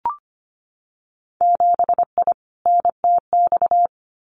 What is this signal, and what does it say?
Morse code, heard as a keyed tone, about 25 wpm, 690 Hz. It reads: 7I NTX
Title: Winter Field Day Exchanges Encore 25wpm